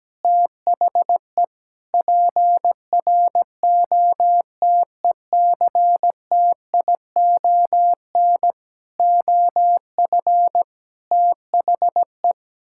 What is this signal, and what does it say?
Morse code, heard as a keyed tone, about 17 wpm, 700 Hz. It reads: THE PROTECTION OF THE
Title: Sentences from Top 1600 Words 17wpm